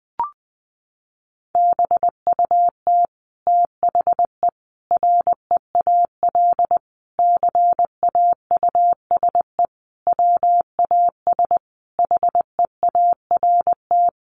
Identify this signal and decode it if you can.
Morse code, heard as a keyed tone, about 20 wpm, 700 Hz. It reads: BUT THE REAL CAUSE WAS HEART
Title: Sentences from Top 1200 Words 20wpm